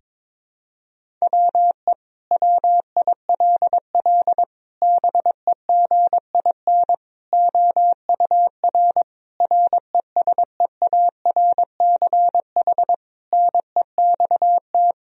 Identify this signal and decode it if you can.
Morse code, heard as a keyed tone, about 22 wpm, 700 Hz. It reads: WE WILL BEGIN OUR RESEARCH NEXT
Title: Sentences from Top 1000 Words - Review Part 4 - 22wpm